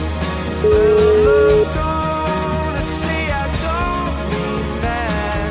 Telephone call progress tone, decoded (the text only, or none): ringback tone